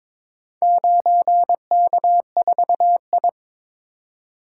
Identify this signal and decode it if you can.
Morse code, heard as a keyed tone, about 22 wpm, 700 Hz. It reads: 9K4I
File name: practice / 22wpm / Four Letter-Number Combinations - 22wpm